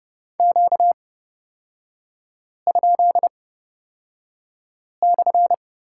Morse code, heard as a keyed tone, about 30 wpm, spaced as for 5 wpm, 700 Hz. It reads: Q?/
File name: Lesson 38e - Letter q - Three Characters 30wpm